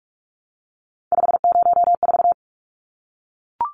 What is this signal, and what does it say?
Morse code, heard as a keyed tone, about 44 wpm, 700 Hz. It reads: 504
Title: Numbers - 3-Digit 45wpm